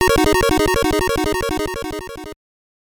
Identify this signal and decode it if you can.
There is no code.